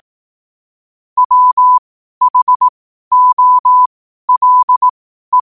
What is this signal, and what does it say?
Morse code, heard as a keyed tone, about 18 wpm, spaced as for 12 wpm, 1000 Hz. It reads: WHOLE
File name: qotdfiles / 12wpm